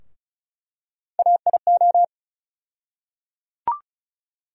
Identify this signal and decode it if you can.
Morse code, heard as a keyed tone, about 36 wpm, 700 Hz. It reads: AIO
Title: USA Call Sign Postfixes - 3 Letters 35wpm